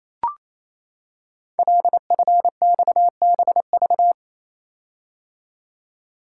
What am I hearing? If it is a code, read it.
Morse code, heard as a keyed tone, about 27 wpm, 700 Hz. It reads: LFXBV
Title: Code Groups - 5 Letters - 28wpm